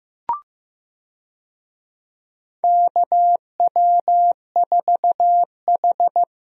Morse code, heard as a keyed tone, about 15 wpm, 700 Hz. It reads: KW4H
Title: USA Call Signs - 4 Characters - Repeated 3x - 15wpm